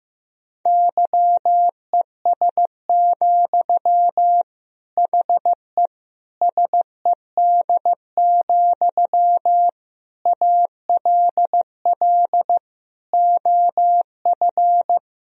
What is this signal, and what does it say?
Morse code, heard as a keyed tone, about 15 wpm, 700 Hz. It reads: YES, HE SED, ALL OF
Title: Sentences from Top 500 Words and Abbreviations 15wpm